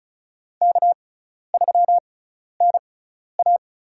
Morse code, heard as a keyed tone, about 36 wpm, spaced as for 12 wpm, 700 Hz. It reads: K3NA
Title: USA Call Signs - 6x Character Spacing - 35wpm